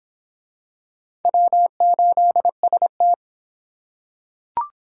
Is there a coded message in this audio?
Yes: Morse code from 1.2 to 3.1 s, heard as a keyed tone, about 25 wpm, 700 Hz. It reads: W8ST